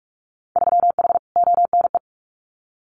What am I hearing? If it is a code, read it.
Morse code, heard as a keyed tone, about 46 wpm, 700 Hz. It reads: 3H ONE
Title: Winter Field Day Exchanges Encore 45wpm